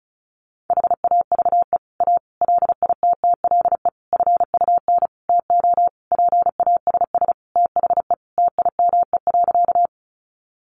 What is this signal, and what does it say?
Morse code, heard as a keyed tone, about 34 wpm, 700 Hz. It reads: HAVE A LITTLE FUN TO PASS THE TIME.
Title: Sentences from Top 700 Words 35wpm